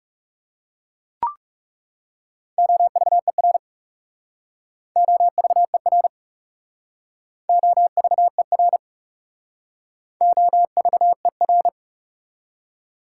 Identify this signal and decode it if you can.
Morse code, heard as a keyed tone, about 41 wpm, 700 Hz. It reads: OVER OVER OVER OVER